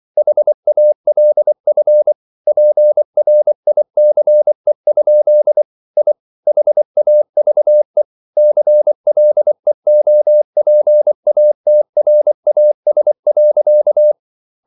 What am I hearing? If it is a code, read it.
Morse code, heard as a keyed tone, about 24 wpm, 600 Hz. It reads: HALF PRICE? I HAVE CLEOPATRAS.